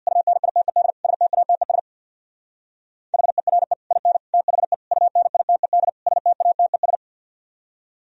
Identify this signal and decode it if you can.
Morse code, heard as a keyed tone, about 58 wpm, 700 Hz. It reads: UNITED STATES HERE IN THE UNITED STATES
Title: Speed-Racing - Sets of 5 Words 50wpm